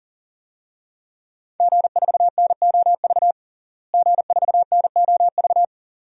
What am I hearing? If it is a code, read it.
Morse code, heard as a keyed tone, about 40 wpm, 700 Hz. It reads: G4NOV G4NOV
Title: DXCC-100 Call Signs - 5 Characters - Repeated 2x - 40wpm